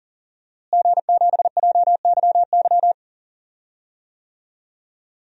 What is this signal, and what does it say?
Morse code, heard as a keyed tone, about 40 wpm, 700 Hz. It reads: G7JYY
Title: DXCC-100 Call Signs - 5 Characters - Repeated 2x - 40wpm